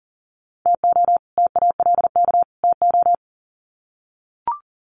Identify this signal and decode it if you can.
Morse code, heard as a keyed tone, about 40 wpm, 700 Hz. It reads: TO TALK TO
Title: Sets of 3 Words 40wpm